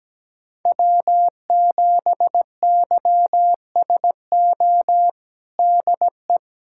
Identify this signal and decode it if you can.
Morse code, heard as a keyed tone, about 17 wpm, 700 Hz. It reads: W7YSO DE
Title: USA Callsign DE Callsign - 5 Characters - 17wpm